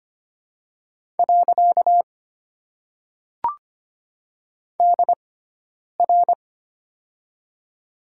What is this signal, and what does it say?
Morse code, heard as a keyed tone, about 25 wpm, 700 Hz. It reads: . D R
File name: Top 90 QSO Elements - 6x Character Spacing - 25wpm